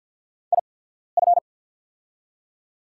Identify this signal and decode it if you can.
Morse code, heard as a keyed tone, about 48 wpm, spaced as for 14 wpm, 690 Hz. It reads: IF
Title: 8x Character Spacing - Top 300 Words - 50wpm